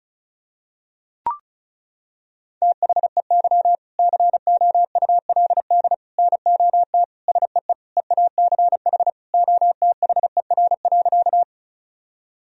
Morse code, heard as a keyed tone, about 36 wpm, 700 Hz. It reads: THEY COULD NOT SEE EACH OTHER.
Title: Sentences from Top 1000 Words - Review Part 1 - 35wpm